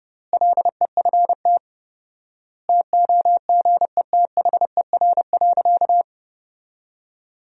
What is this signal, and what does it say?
Morse code, heard as a keyed tone, about 30 wpm, 700 Hz. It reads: LEFT TOGETHER.
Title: 4x Word-Spacing - Sentences from Top 400 Words - 30wpm